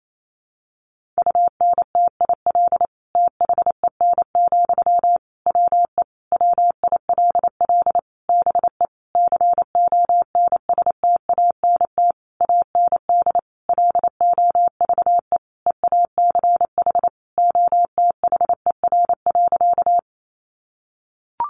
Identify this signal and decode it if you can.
Morse code, heard as a keyed tone, about 28 wpm, 700 Hz. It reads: UNTIL THEN, WE WILL BE CONSTANT AND LOVE EACH OTHER.
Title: Sentences from Top 1500 Words 28wpm